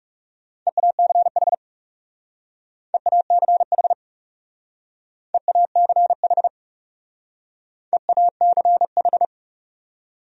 Morse code, heard as a keyed tone, about 38 wpm, 700 Hz. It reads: EACH EACH EACH EACH